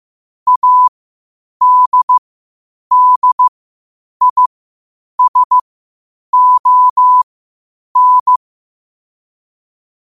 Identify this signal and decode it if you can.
Morse code, heard as a keyed tone, about 15 wpm, spaced as for 8 wpm, 1000 Hz. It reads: ADDISON
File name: 8wpm